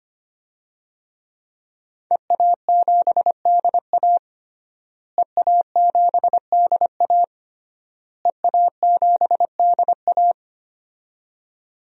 Morse code, heard as a keyed tone, about 25 wpm, 700 Hz. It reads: EA7DA EA7DA EA7DA
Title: DXCC-100 Call Signs - 5 Characters - Repeated 3x - 25wpm